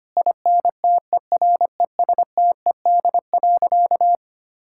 Morse code, heard as a keyed tone, about 25 wpm, 700 Hz. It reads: INTERESTED.
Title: Sentences from Top 1000 Words - Review Part 6 - 25wpm